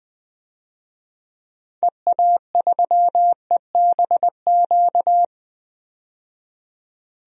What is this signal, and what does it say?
Morse code, heard as a keyed tone, about 20 wpm, 700 Hz. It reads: EA3EBQ
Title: DXCC-100 Call Signs - 6 Characters - 20wpm